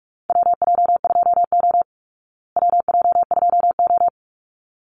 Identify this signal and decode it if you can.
Morse code, heard as a keyed tone, about 45 wpm, 700 Hz. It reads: WJ2O WJ2O
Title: Call Signs Repeated 3x - 45wpm